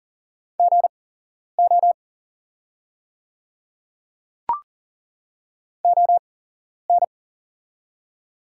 Morse code, heard as a keyed tone, about 40 wpm, spaced as for 11 wpm, 690 Hz. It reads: GO ON
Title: 8x Character Spacing - Top 100 Words - 40wpm